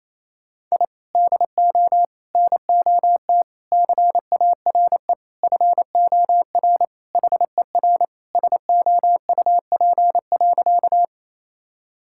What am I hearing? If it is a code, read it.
Morse code, heard as a keyed tone, about 27 wpm, 700 Hz. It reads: I DO NOT CARE FOR HER SOUP.
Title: Sentences from Top 1000 Words - Review Part 6 - 28wpm